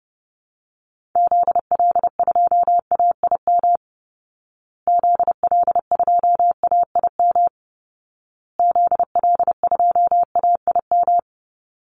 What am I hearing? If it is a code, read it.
Morse code, heard as a keyed tone, about 30 wpm, 700 Hz. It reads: ZL2AIM ZL2AIM ZL2AIM
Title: Call Signs Repeated 2x - 30wpm